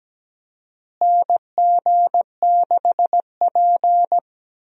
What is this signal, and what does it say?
Morse code, heard as a keyed tone, about 17 wpm, 700 Hz. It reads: NG6P
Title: USA Call Signs - 4 Characters - Repeated 3x - 17wpm